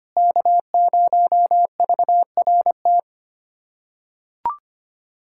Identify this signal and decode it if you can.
Morse code, heard as a keyed tone, about 25 wpm, 700 Hz. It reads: K0VRT
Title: USA Callsign DE Callsign - 5 Characters - 25wpm